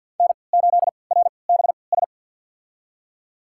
Morse code, heard as a keyed tone, about 47 wpm, spaced as for 27 wpm, 700 Hz. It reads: N8RBS
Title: USA Call Signs - 5 Characters - 3x Character Spacing - 50wpm